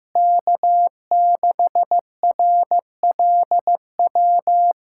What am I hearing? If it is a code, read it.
Morse code, heard as a keyed tone, about 15 wpm, 700 Hz. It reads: K6RLW
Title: USA Callsign DE Callsign - 5 Characters - 15wpm